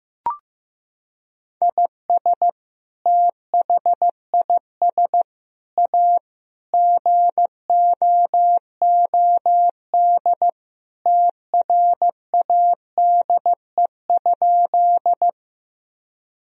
Morse code, heard as a keyed tone, about 15 wpm, 700 Hz. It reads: IS THIS A GOOD TRADE?
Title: Sentences from Top 1000 Words - Review Part 4 - 15wpm